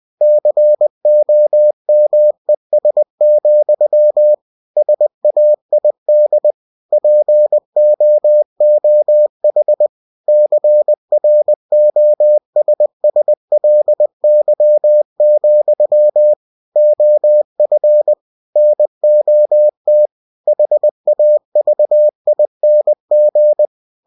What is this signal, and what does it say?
Morse code, heard as a keyed tone, about 20 wpm, 600 Hz. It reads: COMES, SAID POOH CROSSLY, OF NOT HAVING
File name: cw-books / winnie-the-pooh / winnie-20wpm-page-0017